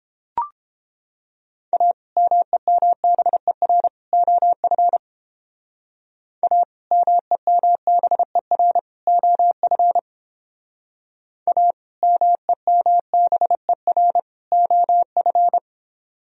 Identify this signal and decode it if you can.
Morse code, heard as a keyed tone, about 30 wpm, 700 Hz. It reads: A MEMBER OF A MEMBER OF A MEMBER OF